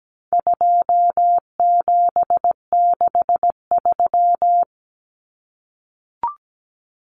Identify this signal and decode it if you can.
Morse code, heard as a keyed tone, about 17 wpm, 700 Hz. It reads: 2763
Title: Common Cut Numbers - 4-Digit 17wpm